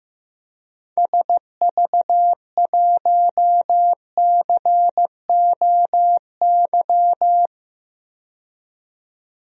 Morse code, heard as a keyed tone, about 15 wpm, 700 Hz. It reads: SV1COY
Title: DXCC-100 Call Signs - 6 Characters - 15wpm